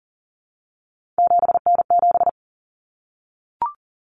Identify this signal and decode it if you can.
Morse code, heard as a keyed tone, about 40 wpm, 700 Hz. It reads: 7N7
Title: Common Cut Numbers - 3-Digit 40wpm